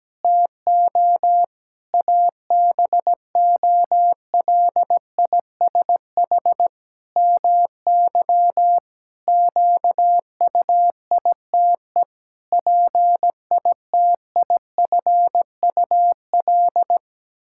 Morse code, heard as a keyed tone, about 17 wpm, 700 Hz. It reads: TO ABOLISH MY QUITE PITIFUL
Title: Pangram Challenge v2 - 17wpm